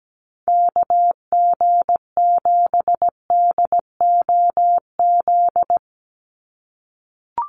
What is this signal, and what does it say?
Morse code, heard as a keyed tone, about 17 wpm, 700 Hz. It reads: KG7DOZ
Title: USA Call Signs 17wpm